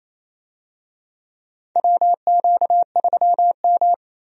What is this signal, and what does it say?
Morse code, heard as a keyed tone, about 28 wpm, 700 Hz. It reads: WQ3M
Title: USA Call Signs - 4 Characters - 28wpm